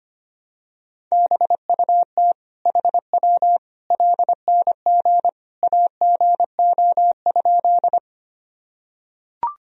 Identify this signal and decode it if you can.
Morse code, heard as a keyed tone, about 25 wpm, 700 Hz. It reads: BUT HW LNG AGO?
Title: Sentences from Top 500 Words and Abbreviations 25wpm